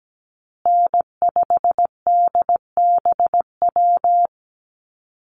Morse code, heard as a keyed tone, about 17 wpm, 700 Hz. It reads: N5DBW
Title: USA Call Signs Repeated 2x 17wpm